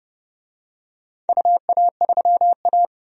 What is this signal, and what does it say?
Morse code, heard as a keyed tone, about 30 wpm, 700 Hz. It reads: UA3A
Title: DXCC-100 Call Signs - 4 Characters - 30wpm